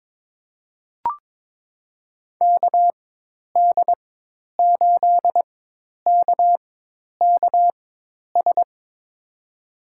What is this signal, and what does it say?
Morse code, heard as a keyed tone, about 22 wpm, spaced as for 10 wpm, 700 Hz. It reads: KD8KKS
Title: USA Call Signs - 6 Characters - 4x Character Spacing - 22wpm